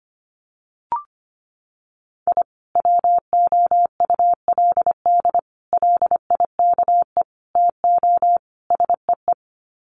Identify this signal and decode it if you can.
Morse code, heard as a keyed tone, about 25 wpm, 700 Hz. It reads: I WOULD LIKE TO SEE
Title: Sets of 5 Words 25wpm